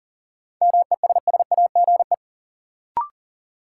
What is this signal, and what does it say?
Morse code, heard as a keyed tone, about 40 wpm, 700 Hz. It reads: MESSAGE
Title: Sentences from Top 1100 Words - New Word Drill - 40wpm